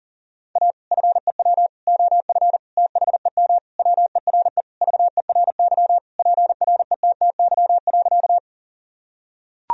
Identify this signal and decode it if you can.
Morse code, heard as a keyed tone, about 40 wpm, 700 Hz. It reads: A FEW OF THEM WERE VERY PRETTY.
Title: Sentences from Top 1000 Words - Review Part 1 - 40wpm